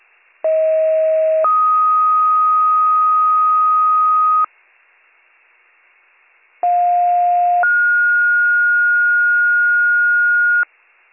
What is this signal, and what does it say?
Two-tone page, heard as a one-second tone, then a three-second tone, about 640 then 1250 Hz, about 710 then 1430 Hz. It is heard from 0.4 to 10.6 s.